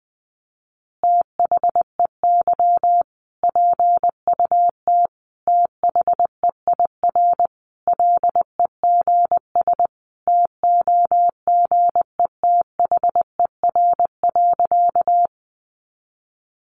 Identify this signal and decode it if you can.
Morse code, heard as a keyed tone, about 20 wpm, 700 Hz. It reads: THEY PUT THEIR LEGS TOGETHER.
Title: Easy Phrases 20wpm